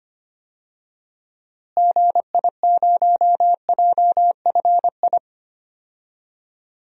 Morse code, heard as a keyed tone, about 25 wpm, 700 Hz. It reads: GI0JFI
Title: DXCC-100 Call Signs - 6 Characters - Repeated 3x - 25wpm